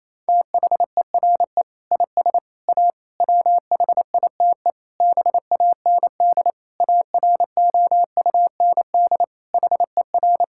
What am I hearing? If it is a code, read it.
Morse code, heard as a keyed tone, about 28 wpm, 700 Hz. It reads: THERE IS A WHITE BAND AROUND HER
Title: Sentences from Top 1600 Words 28wpm